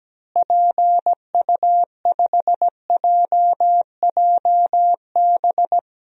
Morse code, heard as a keyed tone, about 17 wpm, 700 Hz. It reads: PU5JJB
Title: DXCC-100 Call Signs - 6 Characters - Repeated 3x - 17wpm